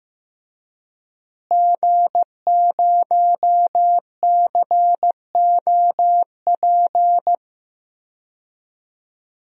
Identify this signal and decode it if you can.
Morse code, heard as a keyed tone, about 15 wpm, 700 Hz. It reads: G0COP
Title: DXCC-100 Call Signs - 5 Characters - Repeated 3x - 15wpm